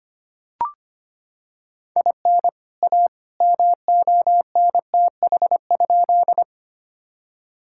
Morse code, heard as a keyed tone, about 25 wpm, 700 Hz. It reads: IN A MONTH?
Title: Sentences from Top 1000 Words - Review Part 5 - 25wpm